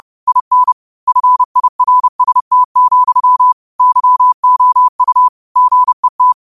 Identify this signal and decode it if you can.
Morse code, heard as a keyed tone, about 30 wpm, 1000 Hz. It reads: IN FIRST, YOU GET